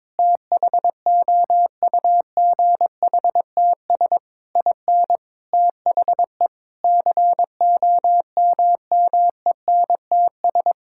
Morse code, heard as a keyed tone, about 22 wpm, 700 Hz. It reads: THOUGHTS IN THE COMMENTS
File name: practice / 22wpm / Sets of 8 Words - 22wpm